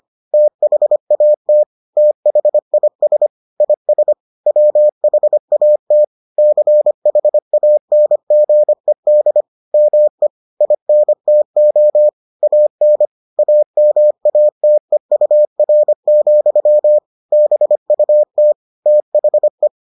Morse code, heard as a keyed tone, about 25 wpm, 600 Hz. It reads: THAT THIS IS WHAT CHANGED ME INTO AN AMATEUR, BUT THE